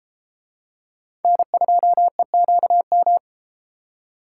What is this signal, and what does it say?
Morse code, heard as a keyed tone, about 31 wpm, 700 Hz. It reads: N2EQM